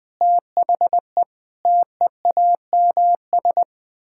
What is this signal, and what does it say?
Morse code, heard as a keyed tone, about 20 wpm, 700 Hz. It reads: THE TEAMS
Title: Sentences with 7-Word Phrases - Spoken Beginning - 20wpm